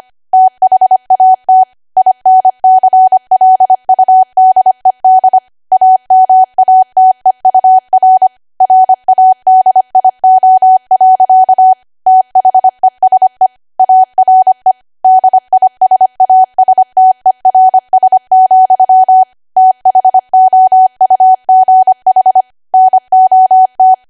Morse code, heard as a keyed tone, about 25 wpm, 750 Hz. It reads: THAT INCLUDED AMATEUR RADIO. THESE ARE DISASTERS, THOUGH NOT
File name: Archive / 25 WPM / 25WPM